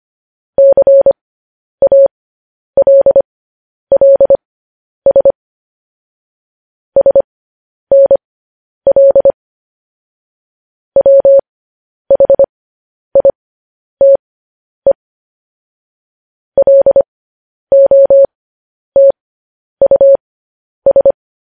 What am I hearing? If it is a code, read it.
Morse code, heard as a keyed tone, about 25 wpm, spaced as for 10 wpm, 550 Hz. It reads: CALLS SNL WHITE LOTUS
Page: news headlines in morse code at 10 wpm